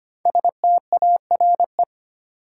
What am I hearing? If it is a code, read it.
Morse code, heard as a keyed tone, about 25 wpm, 700 Hz. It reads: STARE
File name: 5 Letter Words - First 3 Letters Given - 25wpm